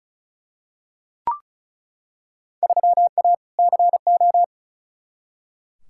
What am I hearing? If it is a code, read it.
Morse code, heard as a keyed tone, about 34 wpm, 700 Hz. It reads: 3A CO
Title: Field Day Exchanges Encore - 35wpm